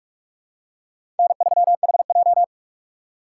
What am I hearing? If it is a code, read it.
Morse code, heard as a keyed tone, about 48 wpm, 690 Hz. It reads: N3HJ
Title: USA Call Signs - 4 Characters - 3x Character Spacing - 45wpm